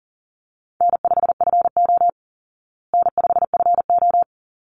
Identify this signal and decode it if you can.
Morse code, heard as a keyed tone, about 37 wpm, 700 Hz. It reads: N5FO N5FO
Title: Call Signs Repeated 3x - 40wpm